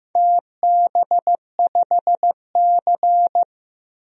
Morse code, heard as a keyed tone, about 16 wpm, 700 Hz. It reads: TB5C
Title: DXCC-100 Call Signs - 4 Characters - Repeated 2x - 15wpm